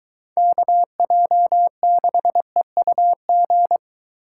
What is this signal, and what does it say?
Morse code, heard as a keyed tone, about 23 wpm, 700 Hz. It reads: KJ6EUG